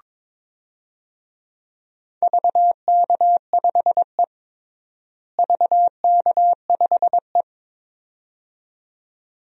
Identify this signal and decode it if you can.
Morse code, heard as a keyed tone, about 22 wpm, 700 Hz. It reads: VK5E VK5E
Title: DXCC-100 Call Signs - 4 Characters - Repeated 2x - 22wpm